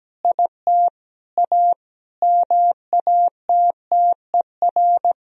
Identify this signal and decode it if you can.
Morse code, heard as a keyed tone, about 17 wpm, 700 Hz. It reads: IT A MATTER
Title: Sentences from Top 1000 Words - Review Part 3 - 17wpm